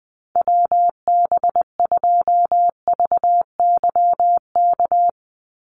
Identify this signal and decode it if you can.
Morse code, heard as a keyed tone, about 20 wpm, 700 Hz. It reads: WB2VYK
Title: Call Signs - 20wpm